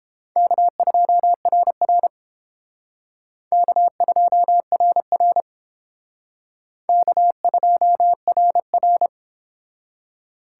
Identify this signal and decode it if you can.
Morse code, heard as a keyed tone, about 31 wpm, 700 Hz. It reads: K2RR K2RR K2RR